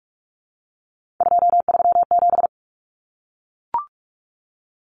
Morse code, heard as a keyed tone, about 44 wpm, 700 Hz. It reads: 237
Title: Numbers - 3-Digit 45wpm